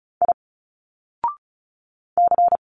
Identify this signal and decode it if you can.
Morse code, heard as a keyed tone, about 35 wpm, 700 Hz. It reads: I C
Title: Single Letters 35wpm